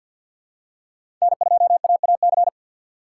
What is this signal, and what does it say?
Morse code, heard as a keyed tone, about 53 wpm, 700 Hz. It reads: N2AAC
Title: USA Call Signs - 5 Characters - 3x Character Spacing - 50wpm